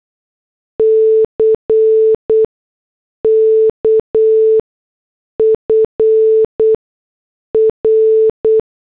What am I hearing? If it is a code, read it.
Morse code, heard as a keyed tone, about 8 wpm, spaced as for 6 wpm, 440 Hz. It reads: CK FR